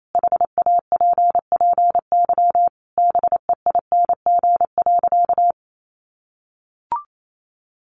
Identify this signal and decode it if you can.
Morse code, heard as a keyed tone, about 28 wpm, 700 Hz. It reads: HAPPY BEING.
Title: Sentences from Top 1000 Words 28wpm